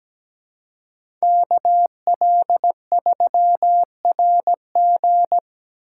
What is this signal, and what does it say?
Morse code, heard as a keyed tone, about 20 wpm, 700 Hz. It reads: KL3RG
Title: Speed Racing - USA 2x2 Call Signs 17wpm